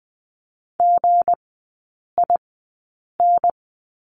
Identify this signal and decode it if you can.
Morse code, heard as a keyed tone, about 20 wpm, spaced as for 8 wpm, 700 Hz. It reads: GIN
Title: 159 Lesson 26e - Letter g - Three Characters 20wpm